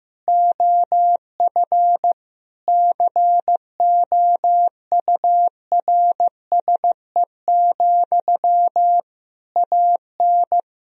Morse code, heard as a keyed tone, about 15 wpm, 700 Hz. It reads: OF COURSE, AN
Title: Sentences from Top 1000 Words - Review Part 3 - 15wpm